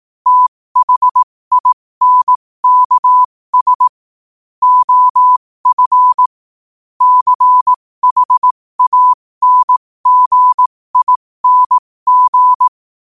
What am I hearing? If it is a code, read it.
Morse code, heard as a keyed tone, about 18 wpm, spaced as for 15 wpm, 1000 Hz. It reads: THINKS OF CHANGING
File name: qotdfiles / 15wpm